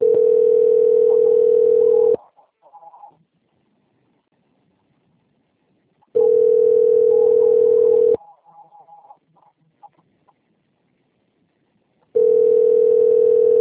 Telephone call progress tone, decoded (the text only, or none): ringback tone